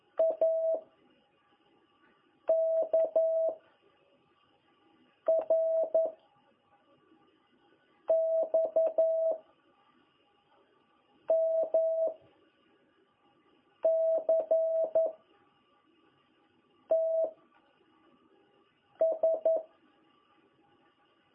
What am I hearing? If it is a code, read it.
Morse code, heard as a keyed tone, about 11 wpm, 660 Hz. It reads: A K R X M C T S
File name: jh8cbh / 2021cw